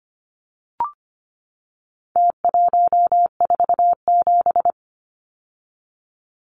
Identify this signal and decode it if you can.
Morse code, heard as a keyed tone, about 25 wpm, 700 Hz. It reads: T147
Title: Numbers - Contest Style - 4-Digit 25wpm